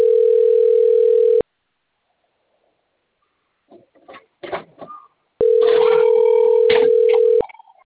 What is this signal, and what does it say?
Ringback tone (North America), 440 + 480 Hz.
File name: rg-1600-unknown- - - .62471